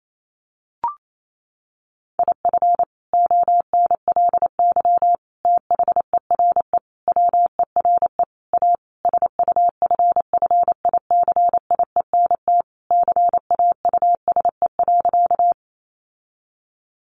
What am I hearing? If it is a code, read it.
Morse code, heard as a keyed tone, about 28 wpm, 700 Hz. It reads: IF ONLY THERE WERE A SUFFICIENT CAUSE.
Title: Sentences from Top 1400 Words 28wpm